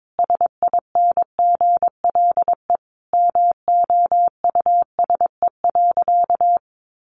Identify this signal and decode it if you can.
Morse code, heard as a keyed tone, about 22 wpm, 700 Hz. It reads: SINGLE MOUSE.
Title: A Sound Of Thunder 22wpm